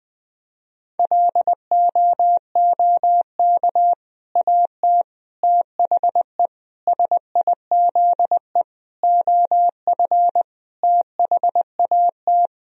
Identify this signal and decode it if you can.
Morse code, heard as a keyed tone, about 20 wpm, 700 Hz. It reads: LOOK AT THE SIZE OF THAT
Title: Sentences from Top 1000 Words - Review Part 3 - 20wpm